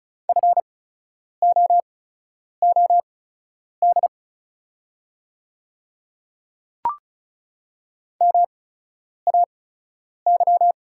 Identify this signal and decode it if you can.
Morse code, heard as a keyed tone, about 36 wpm, spaced as for 10 wpm, 700 Hz. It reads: FOOD MAY
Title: 8x Character Spacing - Top 300 Words - 35wpm